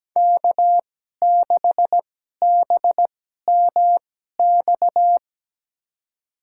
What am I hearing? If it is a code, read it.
Morse code, heard as a keyed tone, about 17 wpm, spaced as for 12 wpm, 700 Hz. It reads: K6BMX
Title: USA Call Signs - 2x Character Spacing - 17wpm